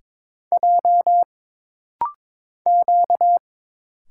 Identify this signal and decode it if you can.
Morse code, heard as a keyed tone, about 22 wpm, 700 Hz. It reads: J Q
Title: Single Letter-Number - Rapid-Fire 22wpm